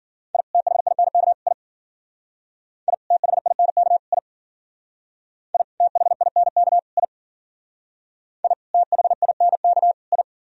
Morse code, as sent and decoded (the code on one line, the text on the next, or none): .. / - .... .. -. -.- / .. / .. / - .... .. -. -.- / .. / .. / - .... .. -. -.- / .. / .. / - .... .. -. -.- / ..
I THINK I I THINK I I THINK I I THINK I